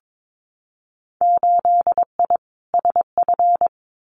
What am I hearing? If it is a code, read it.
Morse code, heard as a keyed tone, about 22 wpm, 700 Hz. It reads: 8I SF
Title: Winter Field Day Exchanges Encore 22wpm